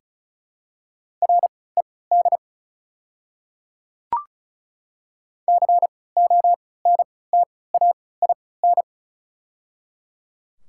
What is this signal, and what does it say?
Morse code, heard as a keyed tone, about 36 wpm, spaced as for 20 wpm, 700 Hz. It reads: RED CONTAIN
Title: 3x Character Spacing - Top 500 Words - 35wpm